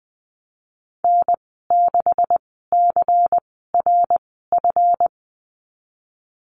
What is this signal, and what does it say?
Morse code, heard as a keyed tone, about 20 wpm, spaced as for 14 wpm, 700 Hz. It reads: N6CRF
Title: 167 Lesson 27g - Number 7 - Call Signs 20wpm